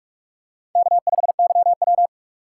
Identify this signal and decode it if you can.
Morse code, heard as a keyed tone, about 44 wpm, 700 Hz. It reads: K5YW